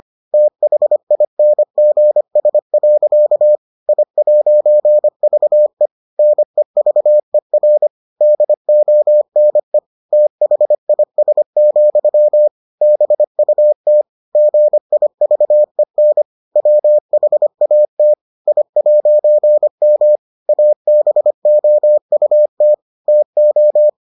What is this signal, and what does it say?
Morse code, heard as a keyed tone, about 25 wpm, 600 Hz. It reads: THINGS. I'VE NEVER DONE THIS, BUT GIVEN WHAT I'M ABOUT TO